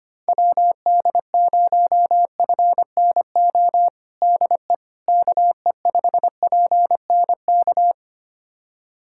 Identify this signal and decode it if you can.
Morse code, heard as a keyed tone, about 25 wpm, 700 Hz. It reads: WD0FNO DE KE5PNK
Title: USA Callsign DE Callsign - 6 Characters - 25wpm